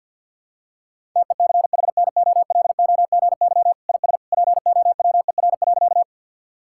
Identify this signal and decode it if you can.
Morse code, heard as a keyed tone, about 52 wpm, 700 Hz. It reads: TECHNOLOGY IS POWER.
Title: Sentences from Top 1000 Words - Review Part 5 - 50wpm